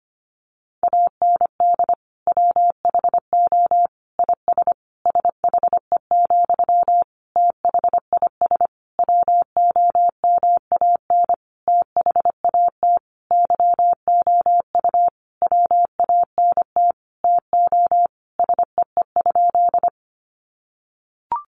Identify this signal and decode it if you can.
Morse code, heard as a keyed tone, about 25 wpm, 700 Hz. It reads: AND WHO IS SHE, THIS WOMAN THAT YOU WANT TO SEE?
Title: Sentences from Top 1100 Words 25wpm